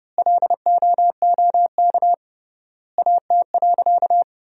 Morse code, heard as a keyed tone, about 30 wpm, 700 Hz. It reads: LOOK AT.
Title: 3x Word-Spacing - Sentences from Top 200 Words - 30wpm